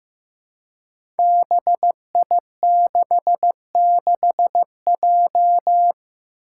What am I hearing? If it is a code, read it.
Morse code, heard as a keyed tone, about 15 wpm, 700 Hz. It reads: BI66J
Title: Five Challenging Letter-Number Combinations 15wpm